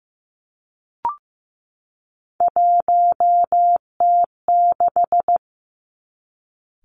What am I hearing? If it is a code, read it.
Morse code, heard as a keyed tone, about 15 wpm, 700 Hz. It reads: 1T6
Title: Three Challenging Letter-Number Combinations 15wpm